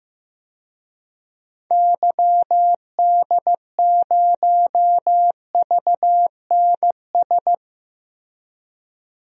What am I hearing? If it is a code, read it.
Morse code, heard as a keyed tone, about 15 wpm, 690 Hz. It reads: YD0VNS
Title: DXCC-100 Call Signs - 6 Characters - 15wpm